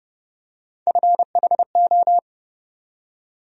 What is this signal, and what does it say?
Morse code, heard as a keyed tone, about 30 wpm, 700 Hz. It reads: FHO
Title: USA Call Sign Postfixes - 3 Letters 30wpm